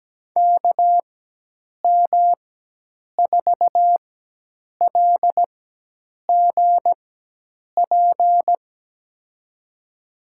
Morse code, heard as a keyed tone, about 17 wpm, spaced as for 8 wpm, 700 Hz. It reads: KM4LGP
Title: USA Call Signs - 6 Characters - 4x Character Spacing - 17wpm